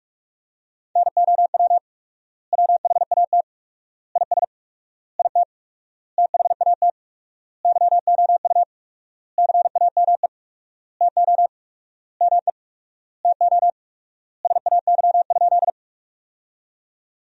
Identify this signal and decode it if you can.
Morse code, heard as a keyed tone, about 41 wpm, 700 Hz. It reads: NOW WHAT IS IT THAT YOU CAME TO ME TO SAY?
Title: 4x Word-Spacing - Sentences from Top 200 Words - 45wpm